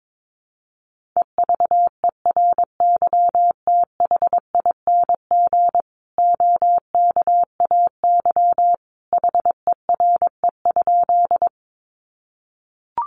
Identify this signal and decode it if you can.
Morse code, heard as a keyed tone, about 22 wpm, 700 Hz. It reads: EVERYTHING OKAY HERE?
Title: A Sound Of Thunder 22wpm